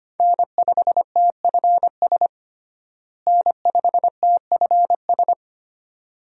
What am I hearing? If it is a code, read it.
Morse code, heard as a keyed tone, about 25 wpm, 700 Hz. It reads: N5TFS N5TFS
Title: USA Call Signs - 5 Characters - Repeated 2x - 25wpm